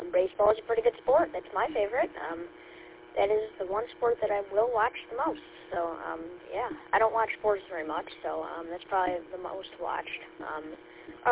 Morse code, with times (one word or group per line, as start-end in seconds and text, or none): none